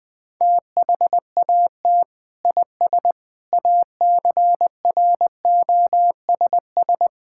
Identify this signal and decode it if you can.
Morse code, heard as a keyed tone, about 20 wpm, 700 Hz. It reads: THAT IS ACROSS
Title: Sentences from Top 1000 Words - Review Part 5 - 20wpm